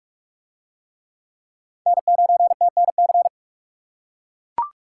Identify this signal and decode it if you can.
Morse code, heard as a keyed tone, about 40 wpm, 700 Hz. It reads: N9TNC